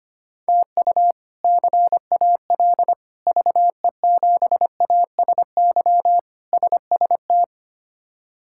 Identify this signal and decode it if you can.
Morse code, heard as a keyed tone, about 25 wpm, 700 Hz. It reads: TU CAL VE7ASY SST
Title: K1USN SST QSO Practice - 25wpm